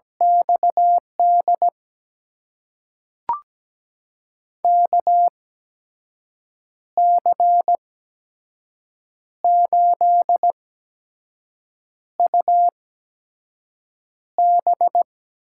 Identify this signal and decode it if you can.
Morse code, heard as a keyed tone, about 17 wpm, spaced as for 5 wpm, 700 Hz. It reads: XD KC8UB
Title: USA Call Signs - 6 Characters - 8x Character Spacing - 17wpm